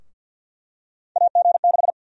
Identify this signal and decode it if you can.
Morse code, heard as a keyed tone, about 51 wpm, 700 Hz. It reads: AG6
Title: USA Call Sign Prefixes - 50wpm